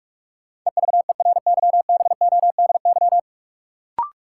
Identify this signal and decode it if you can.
Morse code, heard as a keyed tone, about 41 wpm, 700 Hz. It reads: EVERYBODY
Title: Sentences from Top 1400 Words - New Word Drill - 45wpm